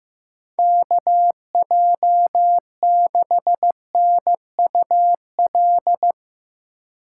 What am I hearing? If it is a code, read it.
Morse code, heard as a keyed tone, about 15 wpm, 700 Hz. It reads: KJ6NUL
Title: USA Call Signs - 6 Characters - 6x Character Spacing - 15wpm